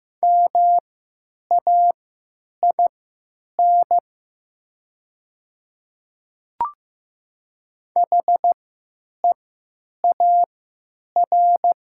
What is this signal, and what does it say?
Morse code, heard as a keyed tone, about 15 wpm, spaced as for 9 wpm, 690 Hz. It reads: MAIN HEAR
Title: 3x Character Spacing - Top 300 Words - 15wpm